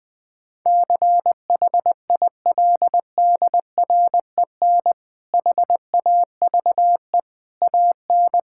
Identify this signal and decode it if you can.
Morse code, heard as a keyed tone, about 20 wpm, 700 Hz. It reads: CHILDREN HAVE AN
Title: Difficult Sentences with Hints - 20wpm